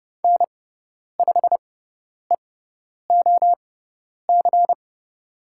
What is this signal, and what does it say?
Morse code, heard as a keyed tone, about 30 wpm, spaced as for 10 wpm, 700 Hz. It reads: N5EOC
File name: Lesson 18g - Letter m - Call Signs 30wpm